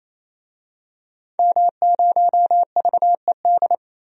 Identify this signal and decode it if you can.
Morse code, heard as a keyed tone, about 28 wpm, 700 Hz. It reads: M0VED
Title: DXCC-100 Call Signs - 5 Characters - 28wpm